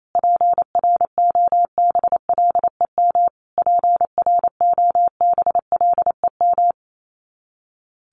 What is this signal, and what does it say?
Morse code, heard as a keyed tone, about 28 wpm, 700 Hz. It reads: PROBLEM PROBLEM
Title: Top 500 Words Repeated 28wpm